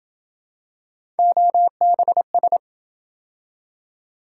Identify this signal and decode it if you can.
Morse code, heard as a keyed tone, about 27 wpm, 700 Hz. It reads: OBS